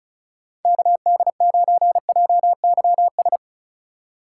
Morse code, heard as a keyed tone, about 34 wpm, 690 Hz. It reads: KD9JYS
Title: USA Call Signs - 6 Characters - 2x Character Spacing - 35wpm